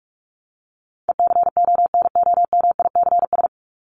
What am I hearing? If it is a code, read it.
Morse code, heard as a keyed tone, about 44 wpm, 700 Hz. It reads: ECONOMICS
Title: Top 501-750 Words 45wpm